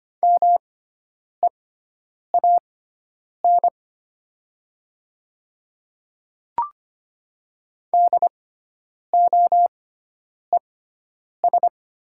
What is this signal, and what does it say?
Morse code, heard as a keyed tone, about 25 wpm, spaced as for 9 wpm, 700 Hz. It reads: MEAN DOES
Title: 6x Character Spacing - Top 200 Words - 25wpm